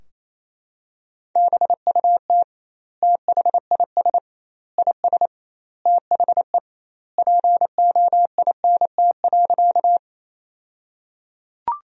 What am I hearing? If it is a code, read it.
Morse code, heard as a keyed tone, about 27 wpm, 700 Hz. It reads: BUT THIS IS THE POINT.
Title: 2x Word-Spacing - Sentences from Top 200 Words - 28wpm